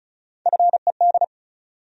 Morse code, heard as a keyed tone, about 36 wpm, 700 Hz. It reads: FED
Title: USA Call Sign Postfixes - 3 Letters 35wpm